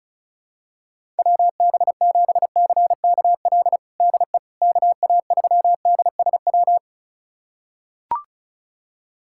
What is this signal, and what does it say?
Morse code, heard as a keyed tone, about 36 wpm, 700 Hz. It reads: WB7CKL DE KA3DSW
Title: USA Callsign DE Callsign - 6 Characters - 35wpm